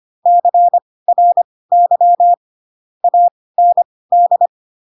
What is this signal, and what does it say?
Morse code, heard as a keyed tone, about 25 wpm, spaced as for 18 wpm, 700 Hz. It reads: CRY AND